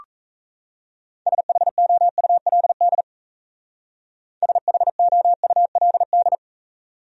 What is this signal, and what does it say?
Morse code, heard as a keyed tone, about 34 wpm, 700 Hz. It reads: SHOULD SHOULD
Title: Speed-Racing Top 500 Words 28wpm